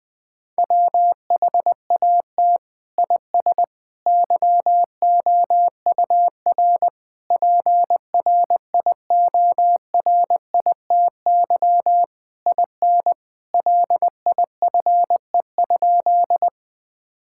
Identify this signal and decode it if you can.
Morse code, heard as a keyed tone, about 20 wpm, 700 Hz. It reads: WHAT IS YOUR PRIORITY IN LIFE?
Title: Sentences from Top 1600 Words 20wpm